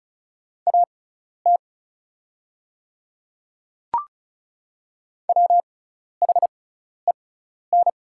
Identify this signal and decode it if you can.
Morse code, heard as a keyed tone, about 34 wpm, spaced as for 12 wpm, 700 Hz. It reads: AT WHEN
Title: 6x Character Spacing - Top 100 Words - 35wpm